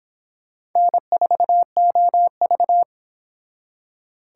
Morse code, heard as a keyed tone, about 26 wpm, 700 Hz. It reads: N4OV